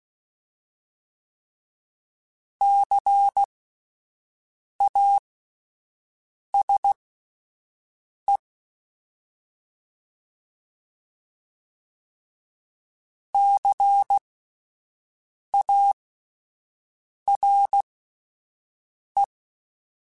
Morse code, heard as a keyed tone, about 16 wpm, spaced as for 5 wpm, 800 Hz. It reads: CASE CARE
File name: Spaced Alphas / Spaced Alphas mp3s / Cs sp @10